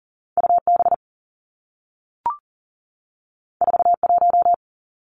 Morse code, heard as a keyed tone, about 39 wpm, 700 Hz. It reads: U B 4 1
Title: Two Challenging Letter-Number Combinations 40wpm